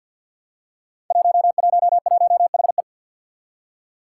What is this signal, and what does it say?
Morse code, heard as a keyed tone, about 47 wpm, 700 Hz. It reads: 111HE
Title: Five Challenging Letter-Number Combinations 50wpm